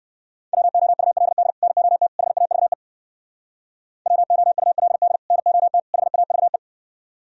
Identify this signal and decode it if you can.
Morse code, heard as a keyed tone, about 58 wpm, 700 Hz. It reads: WOULD NOT HAVE WOULD NOT HAVE